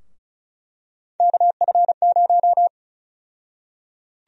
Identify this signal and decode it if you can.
Morse code, heard as a keyed tone, about 34 wpm, 700 Hz. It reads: KF0
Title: USA Call Sign Prefixes - Repeated 2x - 35wpm